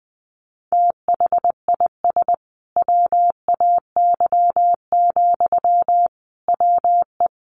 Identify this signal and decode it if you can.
Morse code, heard as a keyed tone, about 21 wpm, 700 Hz. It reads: THIS WAY, WE
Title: A Sound Of Thunder 20wpm